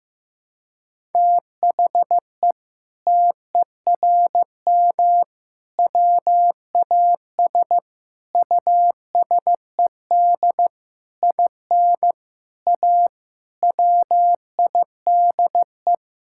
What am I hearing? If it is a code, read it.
Morse code, heard as a keyed tone, about 15 wpm, 700 Hz. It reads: THE TERM WAS USED IN A WIDE
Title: Sentences from Top 1000 Words - Review Part 6 - 15wpm